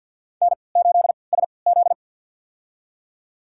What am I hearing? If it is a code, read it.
Morse code, heard as a keyed tone, about 51 wpm, spaced as for 28 wpm, 700 Hz. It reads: N8SZ
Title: USA Call Signs - 4 Characters - 3x Character Spacing - 50wpm